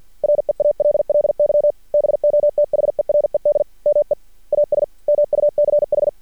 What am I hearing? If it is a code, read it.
Morse code, heard as a keyed tone, about 51 wpm, 590 Hz. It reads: REALLY BOTHERED ME AS MUCH
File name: archived / CRSnet